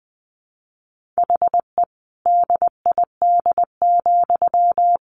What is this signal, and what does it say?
Morse code, heard as a keyed tone, about 20 wpm, 700 Hz. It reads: HE DID,
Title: Sentences from Top 1400 Words 20wpm